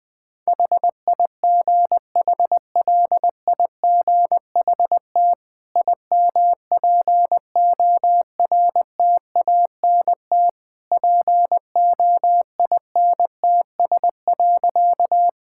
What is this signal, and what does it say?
Morse code, heard as a keyed tone, about 20 wpm, 700 Hz. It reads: HIGHLIGHT IMPORTANT POINTS.
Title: Easy Sentences with Hints - 20wpm